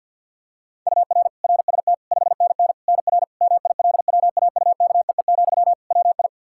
Morse code, heard as a keyed tone, about 50 wpm, 700 Hz. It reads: UR RST 5NN NR MILWAUKEE, WI
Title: Practice QSO - Exchange 1 - Beginning - 50wpm